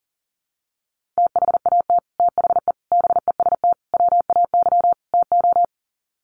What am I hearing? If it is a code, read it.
Morse code, heard as a keyed tone, about 40 wpm, 700 Hz. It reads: THAT THE BEST WAY TO
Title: Sets of 5 Words Encore2 40wpm